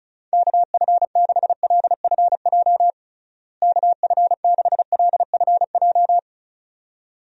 Morse code, heard as a keyed tone, about 36 wpm, 700 Hz. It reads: KF6LFJ KF6LFJ
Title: USA Call Signs - 6 Characters - Repeated 2x - 35wpm